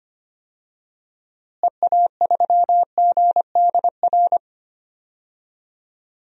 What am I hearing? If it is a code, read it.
Morse code, heard as a keyed tone, about 25 wpm, 700 Hz. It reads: EA3GDR
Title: DXCC-100 Call Signs - 6 Characters - Repeated 2x - 25wpm